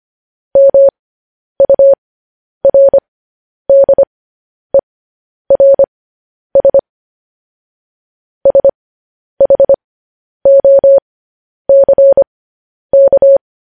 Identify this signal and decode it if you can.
Morse code, heard as a keyed tone, about 25 wpm, spaced as for 10 wpm, 550 Hz. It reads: MURDERS SHOCK
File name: podcast / News headlines-1x-25x10